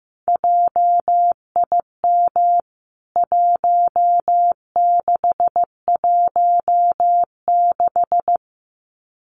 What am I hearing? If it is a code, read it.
Morse code, heard as a keyed tone, about 15 wpm, 700 Hz. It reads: JIM 1616
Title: CWT Exchange 15wpm